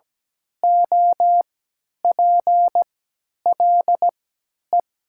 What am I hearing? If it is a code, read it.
Morse code, heard as a keyed tone, about 17 wpm, spaced as for 10 wpm, 700 Hz. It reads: OPLE
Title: 3x Character Spacing - Top 100 Words - 17wpm